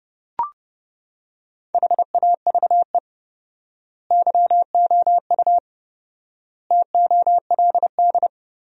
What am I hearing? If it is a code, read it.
Morse code, heard as a keyed tone, about 30 wpm, 700 Hz. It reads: HAVE YOU TOLD
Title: 4x Word-Spacing - Sentences from Top 400 Words - 30wpm